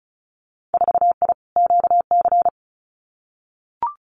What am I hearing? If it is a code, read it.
Morse code, heard as a keyed tone, about 34 wpm, 700 Hz. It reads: 4I QC
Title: Winter Field Day Exchanges Encore 35wpm